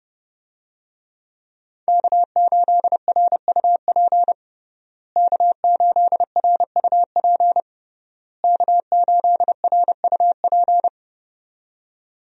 Morse code, heard as a keyed tone, about 29 wpm, 700 Hz. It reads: K8RUP K8RUP K8RUP
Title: USA Call Signs - 5 Characters - Repeated 3x - 30wpm